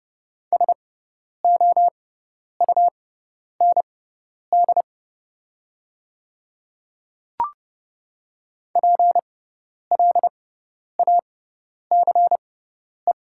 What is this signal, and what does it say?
Morse code, heard as a keyed tone, about 30 wpm, spaced as for 10 wpm, 700 Hz. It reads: SOUND PLACE
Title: 6x Character Spacing - Top 200 Words - 30wpm